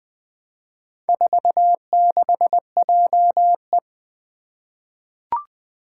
Morse code, heard as a keyed tone, about 20 wpm, 700 Hz. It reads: 46JE
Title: Four Challenging Letter-Number Combinations 20wpm